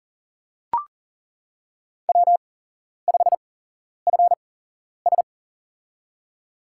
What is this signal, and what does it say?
Morse code, heard as a keyed tone, about 40 wpm, spaced as for 11 wpm, 690 Hz. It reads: W5FS
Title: USA Call Signs - 4 Characters - 8x Character Spacing - 40wpm